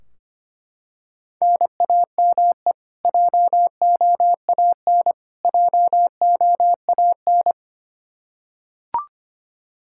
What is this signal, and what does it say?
Morse code, heard as a keyed tone, about 25 wpm, 700 Hz. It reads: NAME JOAN JOAN
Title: Practice QSO - Exchange 1 - Beginning - 25wpm